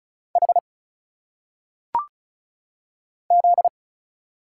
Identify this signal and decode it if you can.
Morse code, heard as a keyed tone, about 35 wpm, 700 Hz. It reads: H Z
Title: Single Letters - Phonetic Pronunciation 35wpm